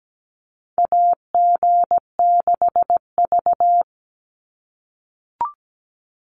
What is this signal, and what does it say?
Morse code, heard as a keyed tone, about 17 wpm, 700 Hz. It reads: AG6V
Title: NA SOTA Chaser Callsigns 17wpm